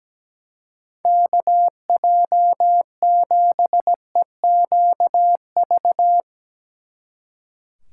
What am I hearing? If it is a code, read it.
Morse code, heard as a keyed tone, about 17 wpm, 700 Hz. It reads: KJ7EQV
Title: Speed Racing - USA 2x3 Call Signs 17wpm